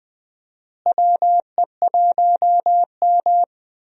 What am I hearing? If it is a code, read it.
Morse code, heard as a keyed tone, about 20 wpm, 700 Hz. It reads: WE1M
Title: USA Call Signs - 4 Characters - Repeated 3x - 20wpm